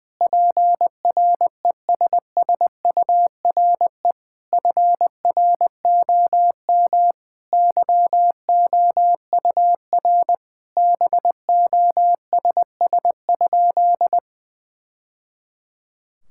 Morse code, heard as a keyed tone, about 20 wpm, 700 Hz. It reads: PRESSURE FROM YOUR BOSS?
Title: Sentences from Top 1000 Words - Review Part 6 - 20wpm